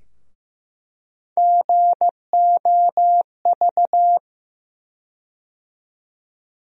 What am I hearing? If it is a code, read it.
Morse code, heard as a keyed tone, about 15 wpm, 700 Hz. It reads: GOV
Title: USA Call Sign Postfixes - 3 Letters 15wpm